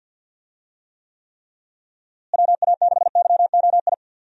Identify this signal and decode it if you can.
Morse code, heard as a keyed tone, about 52 wpm, 700 Hz. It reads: WA6YOI
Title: USA Call Signs - 6 Characters - Repeated 3x - 50wpm